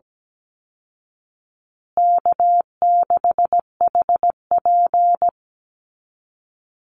Morse code, heard as a keyed tone, about 17 wpm, 700 Hz. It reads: K6HP
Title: Call Signs Repeated 2x - 17wpm